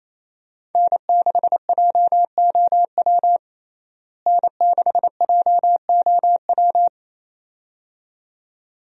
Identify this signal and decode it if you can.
Morse code, heard as a keyed tone, about 28 wpm, 700 Hz. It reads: N6JOW N6JOW
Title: USA Call Signs - 5 Characters - Repeated 2x - 28wpm